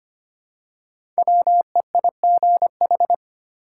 Morse code, heard as a keyed tone, about 25 wpm, 700 Hz. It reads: WEIGH
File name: 5 Letter Words - First 3 Letters Given - 25wpm